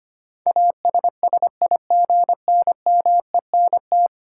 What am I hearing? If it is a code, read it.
Morse code, heard as a keyed tone, about 25 wpm, 700 Hz. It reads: ASSIGNMENT
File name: Ten Letter Words - 25wpm